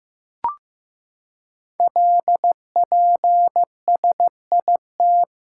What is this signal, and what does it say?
Morse code, heard as a keyed tone, about 15 wpm, 700 Hz. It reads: LPSIT
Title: Code Groups - 5 Letters - 15wpm